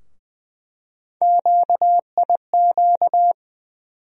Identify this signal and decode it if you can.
Morse code, heard as a keyed tone, about 21 wpm, 700 Hz. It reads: QIQ
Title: USA Call Sign Postfixes - 3 Letters 20wpm